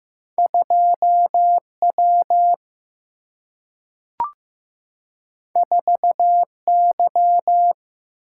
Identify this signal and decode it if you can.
Morse code, heard as a keyed tone, about 15 wpm, 700 Hz. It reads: 2 W 4 Y
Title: Single-Letter Single-Number 15wpm